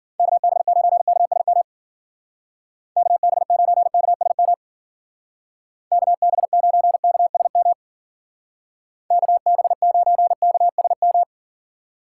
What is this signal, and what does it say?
Morse code, heard as a keyed tone, about 49 wpm, 700 Hz. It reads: KB9KSM KB9KSM KB9KSM KB9KSM
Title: Speed Racing - USA 2x3 Call Signs 40wpm